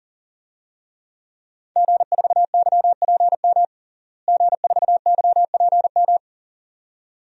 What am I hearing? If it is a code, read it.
Morse code, heard as a keyed tone, about 40 wpm, 700 Hz. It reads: G4YPM G4YPM
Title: DXCC-100 Call Signs - 5 Characters - Repeated 2x - 40wpm